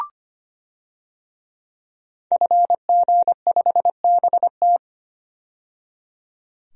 Morse code, heard as a keyed tone, about 25 wpm, 700 Hz. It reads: FG5BT
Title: DXCC-100 Call Signs - 5 Characters - 25wpm